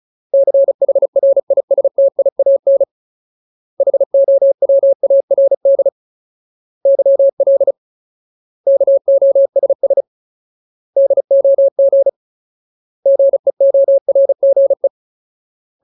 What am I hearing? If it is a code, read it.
Morse code, heard as a keyed tone, about 34 wpm, 550 Hz. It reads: CHRISTIAN HOWARD YL KOSS DOG GEORGE
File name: Ultimate Word List / 35WPM Ultimate